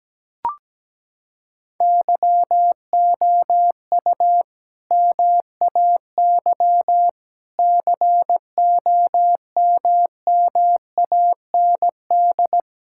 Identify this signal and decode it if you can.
Morse code, heard as a keyed tone, about 17 wpm, 700 Hz. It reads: YOU MAY COMMAND
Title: Sentences from Top 1600 Words 17wpm